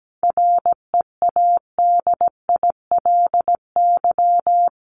Morse code, heard as a keyed tone, about 17 wpm, 700 Hz. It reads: READILY
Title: LY Suffix - Top 100 Words with Base Words 17wpm